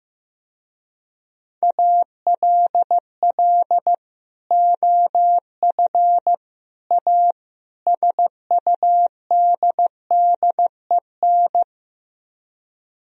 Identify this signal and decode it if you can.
Morse code, heard as a keyed tone, about 16 wpm, 700 Hz. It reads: ALL OF A SUDDEN
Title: Speed-Racing - Sets of 4 Words 15wpm